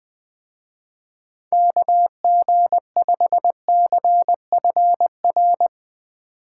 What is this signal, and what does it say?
Morse code, heard as a keyed tone, about 20 wpm, 700 Hz. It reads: KG5CFR
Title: Speed Racing - USA 2x3 Call Signs 17wpm